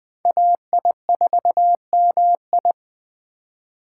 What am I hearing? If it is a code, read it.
Morse code, heard as a keyed tone, about 20 wpm, 700 Hz. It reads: AI4MI
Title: K1USN SST QSO Practice - 20wpm